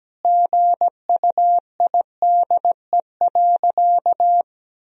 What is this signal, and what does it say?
Morse code, heard as a keyed tone, about 17 wpm, 700 Hz. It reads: GUIDE.
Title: Sentences from Top 1000 Words - Review Part 3 - 17wpm